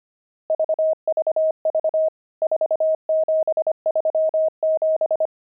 Morse code, heard as a keyed tone, about 25 wpm, 630 Hz. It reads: VVV 4737
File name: pangram-fs-25-cs-25-nums12